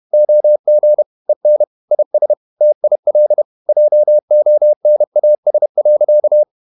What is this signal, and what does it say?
Morse code, heard as a keyed tone, about 31 wpm, 600 Hz. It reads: OG EN IS TIL JONAS.
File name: Ferie QRQ 31wpm